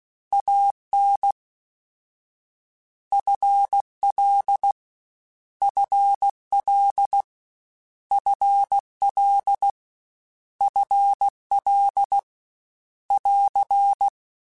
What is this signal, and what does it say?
Morse code, heard as a keyed tone, about 16 wpm, 800 Hz. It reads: AN FL FL FL FL <AR>
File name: Short Runs / Short Runs-mp3s / ShortContrasts @15